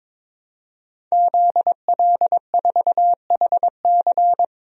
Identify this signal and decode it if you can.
Morse code, heard as a keyed tone, about 22 wpm, 700 Hz. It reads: ZL4HC
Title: DXCC-100 Call Signs - 5 Characters - 22wpm